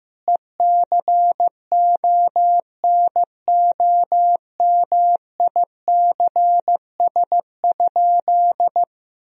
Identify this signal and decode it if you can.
Morse code, heard as a keyed tone, about 15 wpm, 700 Hz. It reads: ECONOMICS?
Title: Sentences from Top 1000 Words - Review Part 3 - 15wpm